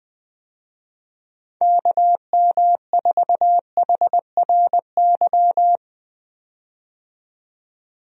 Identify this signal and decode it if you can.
Morse code, heard as a keyed tone, about 21 wpm, 700 Hz. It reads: KM4HRY
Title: USA Call Signs - 6 Characters - 20wpm